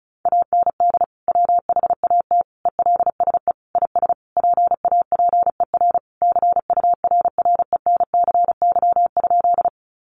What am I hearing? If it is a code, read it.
Morse code, heard as a keyed tone, about 34 wpm, 700 Hz. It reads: AND WHAT ELSE IS PAPER CURRENCY?
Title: Sentences from Top 1500 Words 35wpm